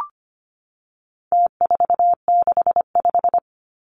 Morse code, heard as a keyed tone, about 25 wpm, 700 Hz. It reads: T465
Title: Numbers - Contest Style - 4-Digit 25wpm